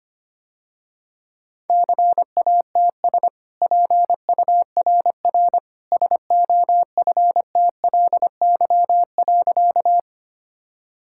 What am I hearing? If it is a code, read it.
Morse code, heard as a keyed tone, about 25 wpm, 700 Hz. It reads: CATS PURR SOFTLY.
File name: Easy Sentences with Hints - 25wpm